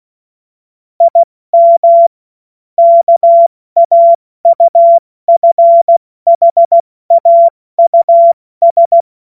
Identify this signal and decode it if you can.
Morse code, heard as a keyed tone, about 16 wpm, spaced as for 14 wpm, 680 Hz. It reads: IM KAUFHAUS
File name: Morsekurs / Witze / Witz 31. August 2021, 70 F80